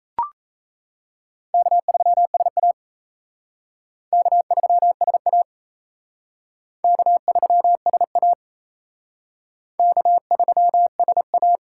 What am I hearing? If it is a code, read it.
Morse code, heard as a keyed tone, about 34 wpm, 700 Hz. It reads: K3SA K3SA K3SA K3SA
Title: Speed Racing - USA 1x2 Call Signs 28wpm